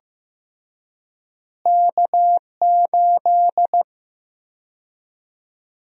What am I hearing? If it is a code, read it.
Morse code, heard as a keyed tone, about 15 wpm, 700 Hz. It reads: K8
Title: USA Call Sign Prefixes - Repeated 2x - 15wpm